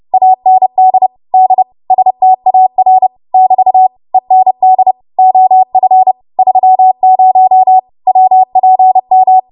Morse code, heard as a keyed tone, about 30 wpm, 750 Hz. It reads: AND D STAR <BT> END OF 30 WPM
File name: Morse / 30WPM